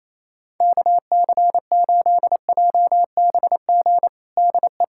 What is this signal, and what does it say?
Morse code, heard as a keyed tone, about 28 wpm, 700 Hz. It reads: KC8JBG DE
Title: USA Callsign DE Callsign - 6 Characters - 28wpm